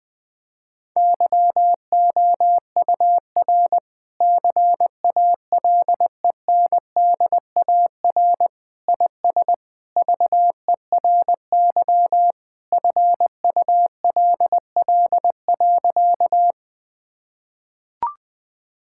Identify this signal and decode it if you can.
Morse code, heard as a keyed tone, about 20 wpm, 700 Hz. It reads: YOUR CALENDAR IS VERY FULL.
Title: Sentences from Top 1600 Words 20wpm